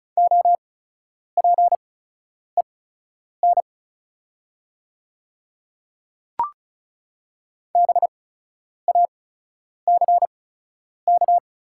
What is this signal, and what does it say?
Morse code, heard as a keyed tone, about 34 wpm, spaced as for 10 wpm, 700 Hz. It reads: OPEN BACK
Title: 8x Character Spacing - Top 300 Words - 35wpm